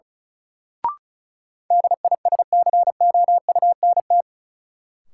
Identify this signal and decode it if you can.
Morse code, heard as a keyed tone, about 36 wpm, 700 Hz. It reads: DISCOUNT
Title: Sentences from Top 1200 Words - New Word Drill Alt - 35wpm